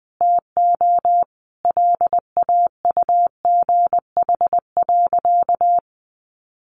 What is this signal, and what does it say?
Morse code, heard as a keyed tone, about 20 wpm, 700 Hz. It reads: TO LAUGH.
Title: Sentences from Top 1100 Words 20wpm